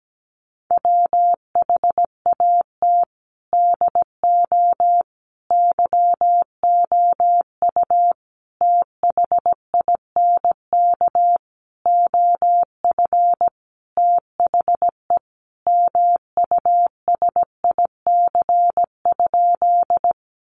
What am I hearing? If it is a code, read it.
Morse code, heard as a keyed tone, about 17 wpm, 700 Hz. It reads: WHAT DO YOU THINK OF THE MUSIC?
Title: Sentences from Top 300 Words 17wpm